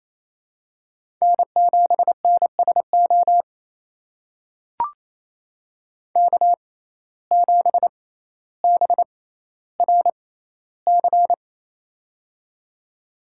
Morse code, heard as a keyed tone, about 28 wpm, spaced as for 14 wpm, 700 Hz. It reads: N7NSO K7BRC
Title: USA Call Signs - 5 Characters - 6x Character Spacing - 28wpm